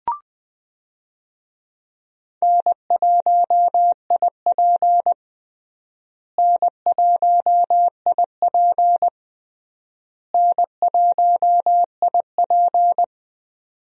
Morse code, heard as a keyed tone, about 20 wpm, 700 Hz. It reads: N1IP N1IP N1IP
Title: USA Call Signs - 4 Characters - Repeated 3x - 20wpm